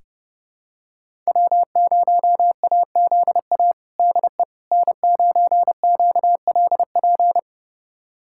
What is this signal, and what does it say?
Morse code, heard as a keyed tone, about 32 wpm, 700 Hz. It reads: W0AZA DE N9QLP
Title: USA Callsign DE Callsign - 5 Characters - 30wpm